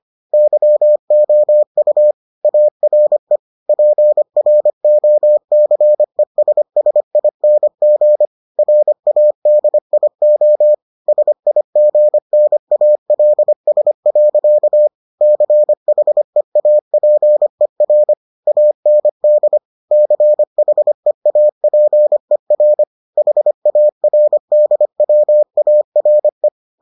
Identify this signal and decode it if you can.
Morse code, heard as a keyed tone, about 25 wpm, 600 Hz. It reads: YOU ARE PROCESSING RADIO SIGNALS. CHEAPER AND CHEAPER HARDWARE